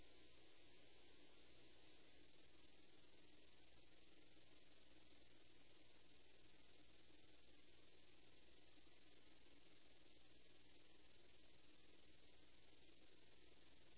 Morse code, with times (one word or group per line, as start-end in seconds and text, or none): none